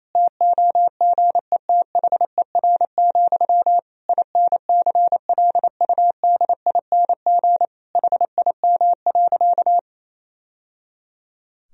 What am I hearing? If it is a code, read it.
Morse code, heard as a keyed tone, about 28 wpm, 700 Hz. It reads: TOGETHER, INCLUDING HIM.
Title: Sentences from Top 1000 Words - Review Part 4 - 28wpm